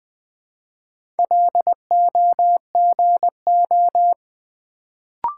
Morse code, heard as a keyed tone, about 20 wpm, 700 Hz. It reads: LOGO
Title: Build A Word - 1 Letter At A Time - 4-Letter Words - 20wpm